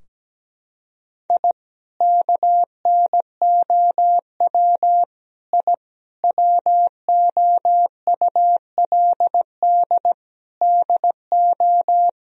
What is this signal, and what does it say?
Morse code, heard as a keyed tone, about 17 wpm, 700 Hz. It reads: I KNOW I WOULD DO
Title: Sentences from Top 1000 Words - Review Part 3 - 17wpm